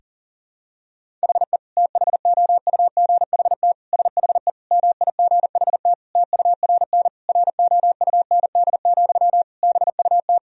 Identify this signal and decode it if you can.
Morse code, heard as a keyed tone, about 40 wpm, 700 Hz. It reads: HE THOUGHT SHE MIGHT TURN ROUND, BUT
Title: Sentences from Top 1000 Words - Review Part 1 - 40wpm